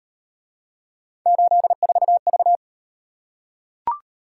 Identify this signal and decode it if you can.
Morse code, heard as a keyed tone, about 38 wpm, 700 Hz. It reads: 84V